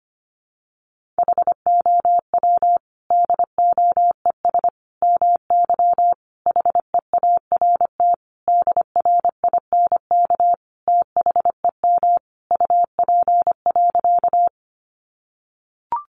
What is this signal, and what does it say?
Morse code, heard as a keyed tone, about 25 wpm, 700 Hz. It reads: HOW DOES MY HEART DRINK THEM UP.
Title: Sentences from Top 1500 Words 25wpm